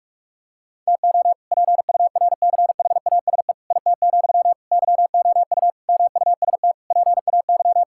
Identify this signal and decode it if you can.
Morse code, heard as a keyed tone, about 46 wpm, 700 Hz. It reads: TO PURCHASE IT, YOU MUST PAY
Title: Sentences from Top 1000 Words - Review Part 5 - 45wpm